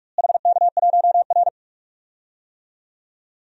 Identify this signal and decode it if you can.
Morse code, heard as a keyed tone, about 44 wpm, 700 Hz. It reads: HK1R
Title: DXCC-100 Call Signs - 4 Characters - Repeated 3x - 45wpm